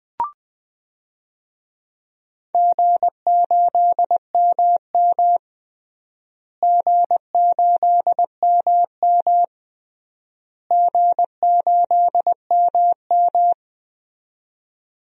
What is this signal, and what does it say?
Morse code, heard as a keyed tone, about 20 wpm, 700 Hz. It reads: G8MM G8MM G8MM
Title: DXCC-100 Call Signs - 4 Characters - Repeated 3x - 20wpm